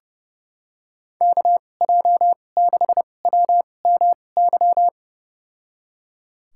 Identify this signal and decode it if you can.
Morse code, heard as a keyed tone, about 29 wpm, spaced as for 21 wpm, 700 Hz. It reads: KJ6WMY
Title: USA Call Signs - 2x Character Spacing - 30wpm